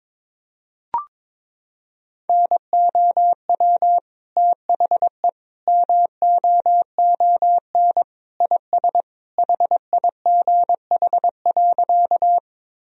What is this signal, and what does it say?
Morse code, heard as a keyed tone, about 22 wpm, 700 Hz. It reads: NOW THE MOON IS HIGH.
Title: Sentences from Top 1000 Words - Review Part 3 - 22wpm